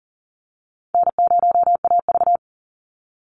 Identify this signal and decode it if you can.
Morse code, heard as a keyed tone, about 40 wpm, 700 Hz. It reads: N0AV
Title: Easy Call Signs - 40wpm